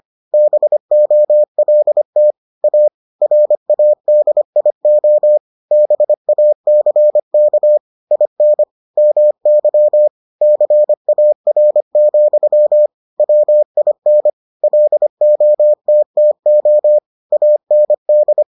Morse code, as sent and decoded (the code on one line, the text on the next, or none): -... --- .-.. - / .- / .-. .- -.. .. --- / -... .- -.-. -.- / .. -. / -- -.-- / -.-. .- .-. --..-- / .-- .. -. / .-.. --- - - --- / .- -. -..
BOLT A RADIO BACK IN MY CAR, WIN LOTTO AND